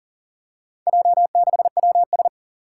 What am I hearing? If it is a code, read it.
Morse code, heard as a keyed tone, about 38 wpm, 700 Hz. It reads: J6WS
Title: Four Challenging Letter-Number Combinations 40wpm